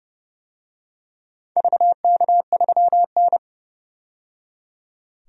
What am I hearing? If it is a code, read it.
Morse code, heard as a keyed tone, about 30 wpm, 700 Hz. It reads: VK3N
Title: DXCC-100 Call Signs - 4 Characters - 30wpm